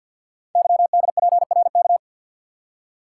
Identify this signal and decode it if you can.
Morse code, heard as a keyed tone, about 50 wpm, 690 Hz. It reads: YDPRK
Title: Code Groups - 5 Letters - 50wpm